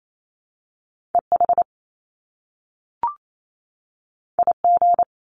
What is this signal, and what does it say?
Morse code, heard as a keyed tone, about 28 wpm, 700 Hz. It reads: EH IG
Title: Top Two Letter Combinations in Words 28wpm